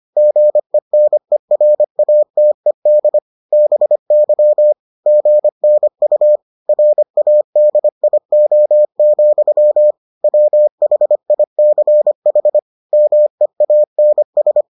Morse code, heard as a keyed tone, about 25 wpm, 600 Hz. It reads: GENERATED BY GNU RADIO, WHICH MEANS